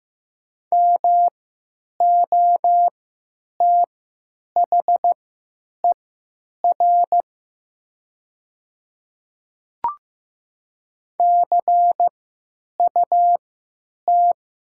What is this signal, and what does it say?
Morse code, heard as a keyed tone, about 16 wpm, spaced as for 9 wpm, 700 Hz. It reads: MOTHER CUT
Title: 3x Character Spacing - Top 300 Words - 15wpm